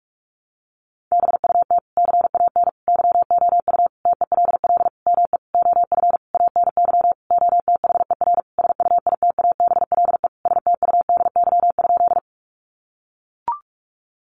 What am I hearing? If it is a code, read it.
Morse code, heard as a keyed tone, about 44 wpm, 700 Hz. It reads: BUT CAN YOU TELL ME OF ANY OTHER SUITABLE STUDY?
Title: Sentences from Top 1400 Words 45wpm